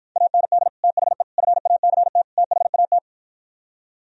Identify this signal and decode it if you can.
Morse code, heard as a keyed tone, about 56 wpm, 700 Hz. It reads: AND THE FACT THAT